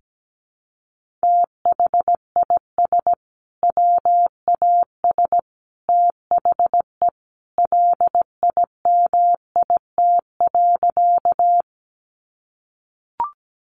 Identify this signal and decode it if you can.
Morse code, heard as a keyed tone, about 17 wpm, 700 Hz. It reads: THIS WAS THE LIMIT.
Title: Sentences from Top 1500 Words 17wpm